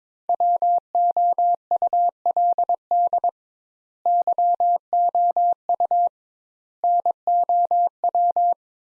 Morse code, heard as a keyed tone, about 22 wpm, 700 Hz. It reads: WOULD YOU NOW
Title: Sentences Top 200 Words - 22wpm